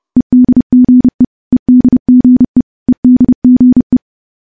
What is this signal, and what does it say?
Morse code, heard as a keyed tone, about 31 wpm, 250 Hz. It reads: EDGE EDGE EDGE
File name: morse / morse-training-30WPM